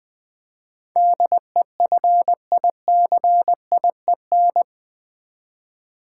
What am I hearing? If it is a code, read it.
Morse code, heard as a keyed tone, about 20 wpm, 700 Hz. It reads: DEFICIEN
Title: Build A Word - 4 Letters At A Time - 10-Letter Words - 20wpm